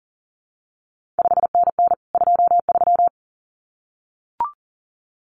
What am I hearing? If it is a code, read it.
Morse code, heard as a keyed tone, about 39 wpm, 700 Hz. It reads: 5NN 23
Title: CQ WW DX Contest Exchange Encore - 40wpm